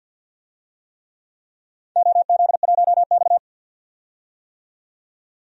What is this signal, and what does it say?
Morse code, heard as a keyed tone, about 53 wpm, 700 Hz. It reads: OZ1X
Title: DXCC-100 Call Signs - 4 Characters - 50wpm